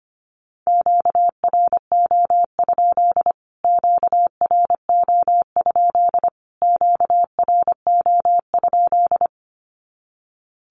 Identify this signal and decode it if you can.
Morse code, heard as a keyed tone, about 25 wpm, 700 Hz. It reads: QRO? QRO? QRO?
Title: Q-Codes Repeated 3x 25wpm